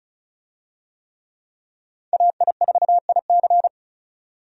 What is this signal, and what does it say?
Morse code, heard as a keyed tone, about 34 wpm, 700 Hz. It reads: AI4IC
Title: USA Call Signs - 5 Characters - 35wpm